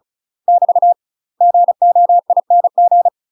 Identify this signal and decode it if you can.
Morse code, heard as a keyed tone, about 36 wpm, 700 Hz. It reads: <BT> GOING
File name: UK MASTER QSOs TRAINING / 35WPM / UK MASTER QSOs 0026